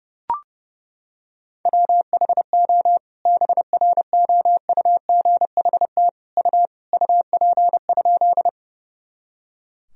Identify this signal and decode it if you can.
Morse code, heard as a keyed tone, about 32 wpm, 700 Hz. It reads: WHO BROUGHT U UP?
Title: Sentences from Top 500 Words and Abbreviations 30wpm